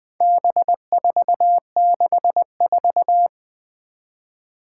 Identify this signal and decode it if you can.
Morse code, heard as a keyed tone, about 20 wpm, 700 Hz. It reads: B464
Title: Four Challenging Letter-Number Combinations 20wpm